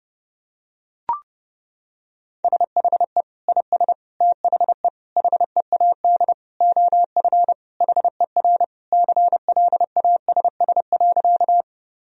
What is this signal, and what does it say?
Morse code, heard as a keyed tone, about 32 wpm, 700 Hz. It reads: SHE IS THE HEAD OF HER CLASS.
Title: Sentences from Top 1000 Words - Review Part 2 - 30wpm